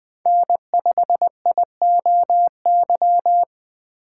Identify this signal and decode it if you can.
Morse code, heard as a keyed tone, about 20 wpm, 700 Hz. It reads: N5IOY
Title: USA Call Signs - 5 Characters - 4x Character Spacing - 20wpm